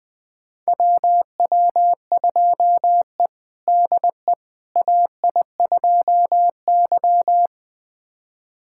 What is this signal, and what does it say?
Morse code, heard as a keyed tone, about 20 wpm, 700 Hz. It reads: WW2E DE AI2Y
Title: USA Callsign DE Callsign - 4 Characters - 20wpm